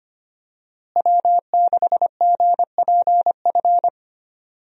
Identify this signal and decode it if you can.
Morse code, heard as a keyed tone, about 25 wpm, 700 Hz. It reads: W6GPF
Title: USA Call Signs - 5 Characters - Repeated 3x - 25wpm